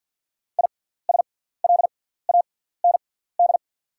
Morse code, heard as a keyed tone, about 48 wpm, spaced as for 17 wpm, 690 Hz. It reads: ISLAND
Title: 6x Character Spacing - Top 500 Words - 50wpm